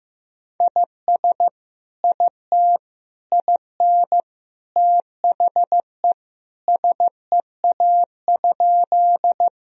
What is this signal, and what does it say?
Morse code, heard as a keyed tone, about 15 wpm, 700 Hz. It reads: IS IT IN THE SEA?
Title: Sentences from Top 1000 Words - Review Part 1 - 15wpm